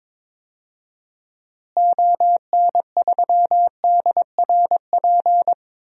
Morse code, heard as a keyed tone, about 22 wpm, 700 Hz. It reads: ON3DRP
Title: DXCC-100 Call Signs - 6 Characters - Repeated 2x - 22wpm